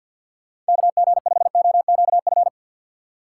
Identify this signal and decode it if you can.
Morse code, heard as a keyed tone, about 52 wpm, 700 Hz. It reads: KG5OQF
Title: USA Call Signs - 6 Characters - 6x Character Spacing - 50wpm